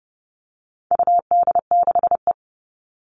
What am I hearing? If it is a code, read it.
Morse code, heard as a keyed tone, about 30 wpm, 700 Hz. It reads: UD6E
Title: All Cut Numbers - 4-Digit 30wpm